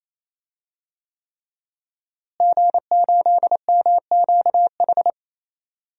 Morse code, heard as a keyed tone, about 27 wpm, 700 Hz. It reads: G8MQH
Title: DXCC-100 Call Signs - 5 Characters - Repeated 3x - 28wpm